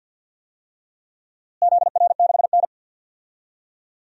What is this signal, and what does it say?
Morse code, heard as a keyed tone, about 53 wpm, 700 Hz. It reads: ZR6N
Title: DXCC-100 Call Signs - 4 Characters - Repeated 3x - 50wpm